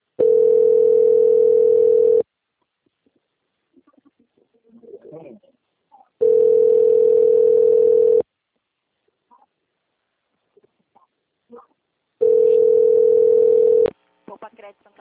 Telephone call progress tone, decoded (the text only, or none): ringback tone